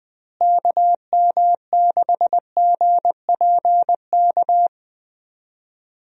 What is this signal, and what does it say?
Morse code, heard as a keyed tone, about 20 wpm, 700 Hz. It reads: KM6GPK
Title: USA Call Signs - 6 Characters - Repeated 2x - 20wpm